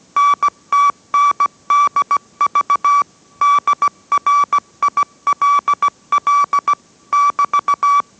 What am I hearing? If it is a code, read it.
Morse code, heard as a keyed tone, about 17 wpm, 1150 Hz. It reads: NTNDV DRILL <BT>